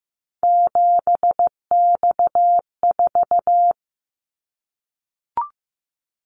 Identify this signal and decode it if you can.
Morse code, heard as a keyed tone, about 15 wpm, 700 Hz. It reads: 7X4
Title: Callsign Prefixes for Top 100 DXCC - 15wpm